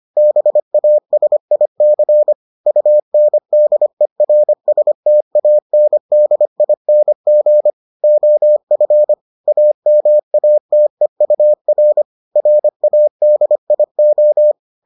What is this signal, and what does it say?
Morse code, heard as a keyed tone, about 25 wpm, 600 Hz. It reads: BASIC UNDERSTANDING OF AMATEUR RADIO